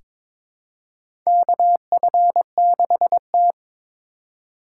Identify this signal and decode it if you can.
Morse code, heard as a keyed tone, about 22 wpm, 700 Hz. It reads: KF6T
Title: K1USN SST QSO Practice - 22wpm